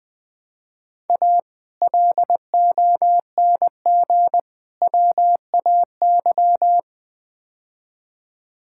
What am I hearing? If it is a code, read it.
Morse code, heard as a keyed tone, about 20 wpm, 700 Hz. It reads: A LONG WAY
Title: Build A Phrase - 5-Word Phrases - 20wpm